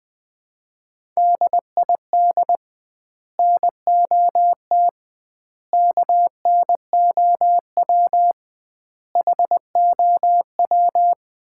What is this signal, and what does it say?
Morse code, heard as a keyed tone, about 20 wpm, 700 Hz. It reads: DID NOT KNOW HOW
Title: 2x Word-Spacing - Sets of 5 Words - 20wpm